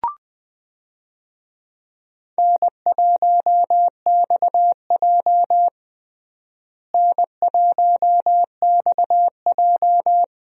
Morse code, heard as a keyed tone, about 20 wpm, 700 Hz. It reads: N1XJ N1XJ
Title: USA Call Signs - 4 Characters - Repeated 2x - 20wpm